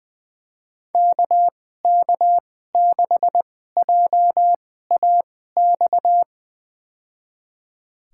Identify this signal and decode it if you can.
Morse code, heard as a keyed tone, about 20 wpm, spaced as for 14 wpm, 700 Hz. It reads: KK6JAX
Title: USA Call Signs - 2x Character Spacing - 20wpm